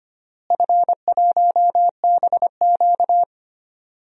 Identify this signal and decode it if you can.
Morse code, heard as a keyed tone, about 25 wpm, 700 Hz. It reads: F1BQ
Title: DXCC-100 Call Signs - 4 Characters - Repeated 2x - 25wpm